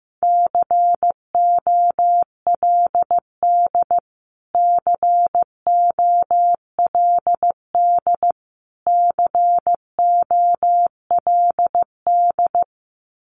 Morse code, heard as a keyed tone, about 15 wpm, 700 Hz. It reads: COLD COLD COLD
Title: Top 500 Words Repeated 15wpm